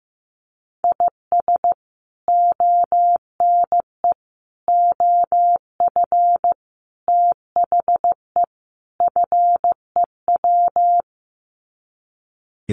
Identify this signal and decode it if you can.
Morse code, heard as a keyed tone, about 15 wpm, 700 Hz. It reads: IS ONE OF THE FEW
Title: Sets of 5 Words 15wpm